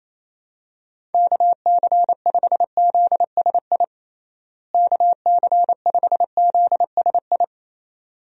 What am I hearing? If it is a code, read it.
Morse code, heard as a keyed tone, about 28 wpm, 700 Hz. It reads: KC5ZSI KC5ZSI
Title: USA Call Signs - 6 Characters - Repeated 2x - 28wpm